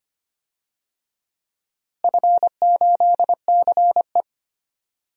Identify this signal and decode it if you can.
Morse code, heard as a keyed tone, about 25 wpm, 700 Hz. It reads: F8CE
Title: DXCC-100 Call Signs - 4 Characters - Repeated 3x - 25wpm